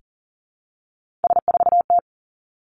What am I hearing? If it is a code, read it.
Morse code, heard as a keyed tone, about 40 wpm, 700 Hz. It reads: S4T
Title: Three Challenging Letter-Number Combinations 40wpm